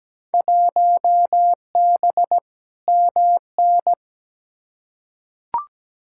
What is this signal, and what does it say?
Morse code, heard as a keyed tone, about 17 wpm, 700 Hz. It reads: 1B MN
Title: Field Day Exchanges Encore - 17wpm